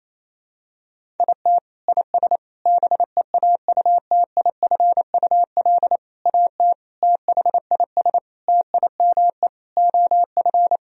Morse code, heard as a keyed tone, about 28 wpm, 700 Hz. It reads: IT IS BEAUTIFUL AT THIS TIME OF
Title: Sentences from Top 1000 Words - Review Part 1 - 28wpm